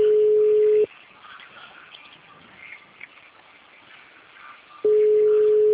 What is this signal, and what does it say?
Ringback tone (Europe, 425 Hz).
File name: out- -2017- - - .40161